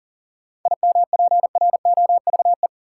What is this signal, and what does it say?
Morse code, heard as a keyed tone, about 40 wpm, 700 Hz. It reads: IMPROVE
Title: Sentences from Top 800 Words - New Word Drill - 40wpm